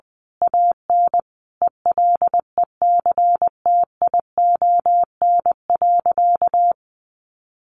Morse code, heard as a keyed tone, about 19 wpm, 700 Hz. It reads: AN ELECTION.
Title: Sentences from Top 1500 Words 20wpm